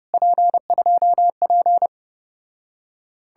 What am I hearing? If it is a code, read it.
Morse code, heard as a keyed tone, about 35 wpm, 700 Hz. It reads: P2P